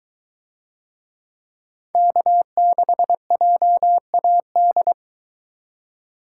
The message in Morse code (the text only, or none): K6JAD